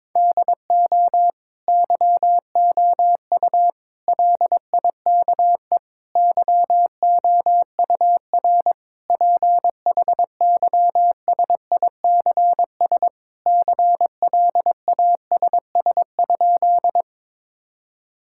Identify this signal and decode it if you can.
Morse code, heard as a keyed tone, about 21 wpm, 700 Hz. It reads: DO YOU LIKE YOUR PHYSICS CLASS?
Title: Sentences from Top 1000 Words - Review Part 5 - 22wpm